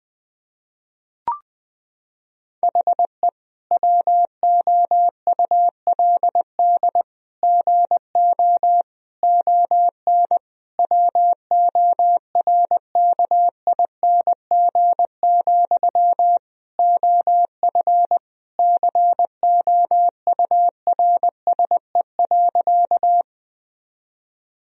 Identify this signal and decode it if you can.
Morse code, heard as a keyed tone, about 20 wpm, 700 Hz. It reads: HE WOULD GO ON WORKING, OF COURSE.
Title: Sentences from Top 1000 Words - Review Part 4 - 20wpm